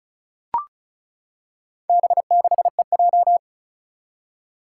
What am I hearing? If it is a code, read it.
Morse code, heard as a keyed tone, about 36 wpm, 700 Hz. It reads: B6EJ
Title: Four Challenging Letter-Number Combinations 35wpm